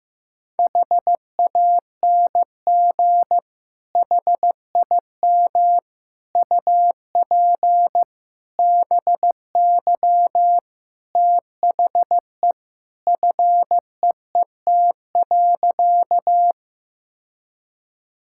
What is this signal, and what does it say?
Morse code, heard as a keyed tone, about 15 wpm, 700 Hz. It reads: HANG HIM UP BY THE FEET.
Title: Sentences from Top 1600 Words 15wpm